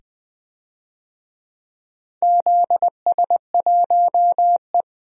Morse code, heard as a keyed tone, about 20 wpm, 700 Hz. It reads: ZS1E
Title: DXCC-100 Call Signs - 4 Characters - 20wpm